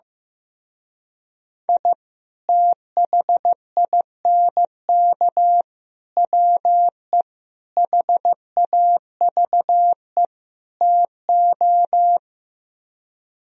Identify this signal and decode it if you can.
Morse code, heard as a keyed tone, about 15 wpm, 700 Hz. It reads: I THINK WE HAVE TO
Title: Speed-Racing - Sets of 5 Words 15wpm